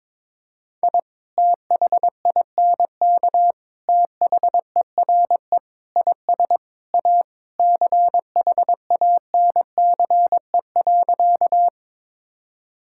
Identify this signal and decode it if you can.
Morse code, heard as a keyed tone, about 22 wpm, 700 Hz. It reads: I THINK THERE IS A CHANCE.
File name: practice / 22wpm / Sentences with 3-Word Phrases - 22wpm